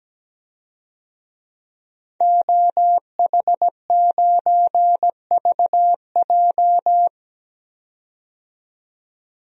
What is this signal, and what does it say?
Morse code, heard as a keyed tone, about 17 wpm, 700 Hz. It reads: OH9VJ
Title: DXCC-100 Call Signs - 5 Characters - 17wpm